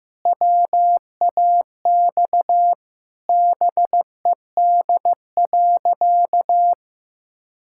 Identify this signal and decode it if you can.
Morse code, heard as a keyed tone, about 15 wpm, 700 Hz. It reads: WAX BED.
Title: Pangram Challenge v2 - 15wpm